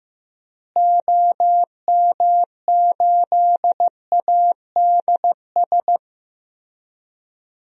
Morse code, heard as a keyed tone, about 15 wpm, 700 Hz. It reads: OM8ADS
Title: DXCC-100 Call Signs - 6 Characters - Repeated 3x - 15wpm